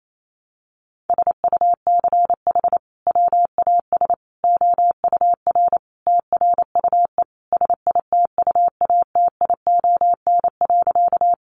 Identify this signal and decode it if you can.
Morse code, heard as a keyed tone, about 28 wpm, 700 Hz. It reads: SUCH WAS OUR TRUE SITUATION.
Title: Sentences from Top 1200 Words 28wpm